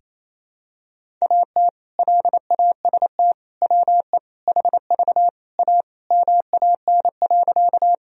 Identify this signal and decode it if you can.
Morse code, heard as a keyed tone, about 28 wpm, 700 Hz. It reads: AT LAST WE HV A MAN.
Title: Sentences from Top 300 Words and Abbreviations 28wpm